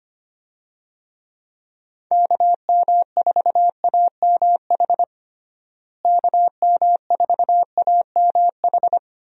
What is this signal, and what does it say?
Morse code, heard as a keyed tone, about 25 wpm, 700 Hz. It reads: KM4AMH KM4AMH
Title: USA Call Signs - 6 Characters - Repeated 3x - 25wpm